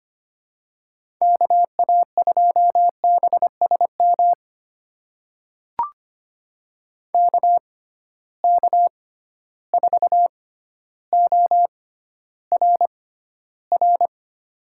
Morse code, heard as a keyed tone, about 25 wpm, spaced as for 13 wpm, 700 Hz. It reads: KA2BSM KK4ORR
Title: USA Call Signs - 6 Characters - 6x Character Spacing - 25wpm